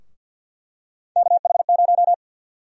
Morse code, heard as a keyed tone, about 50 wpm, 700 Hz. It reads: KH0
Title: USA Call Sign Prefixes - 50wpm